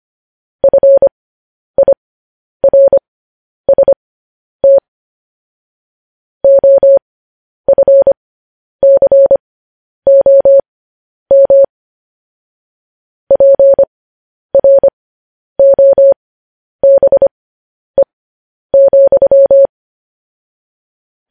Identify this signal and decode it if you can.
Morse code, heard as a keyed tone, about 25 wpm, spaced as for 10 wpm, 550 Hz. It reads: FIRST OFCOM PROBE,
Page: news headlines in morse code at 10 wpm